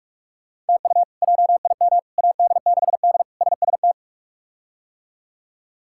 Morse code, heard as a keyed tone, about 42 wpm, 700 Hz. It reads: TU JIM AD6D SST
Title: K1USN SST QSO Practice - 45wpm